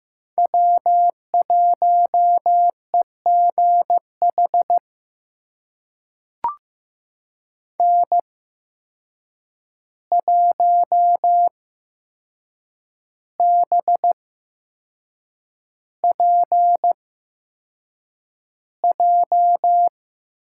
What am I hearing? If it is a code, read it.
Morse code, heard as a keyed tone, about 15 wpm, spaced as for 6 wpm, 700 Hz. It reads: W1EGH N1BPJ
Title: USA Call Signs - 5 Characters - 8x Character Spacing - 15wpm